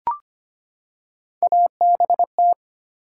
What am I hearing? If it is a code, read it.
Morse code, heard as a keyed tone, about 25 wpm, 700 Hz. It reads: ABT
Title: Top 90 QSO Elements 25wpm